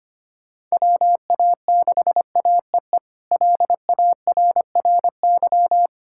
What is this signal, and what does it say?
Morse code, heard as a keyed tone, about 25 wpm, 700 Hz. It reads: WA6AEE LARRY
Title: K1USN SST QSO Practice - 25wpm